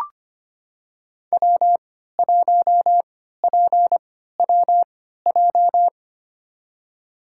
Morse code, heard as a keyed tone, about 25 wpm, spaced as for 14 wpm, 700 Hz. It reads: W1PWJ
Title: USA Call Signs - 5 Characters - 3x Character Spacing - 25wpm